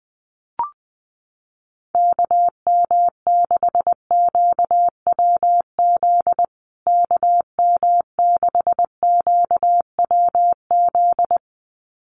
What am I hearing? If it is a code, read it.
Morse code, heard as a keyed tone, about 20 wpm, 700 Hz. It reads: KM6QWZ KM6QWZ
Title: USA Call Signs Repeated 2x 20wpm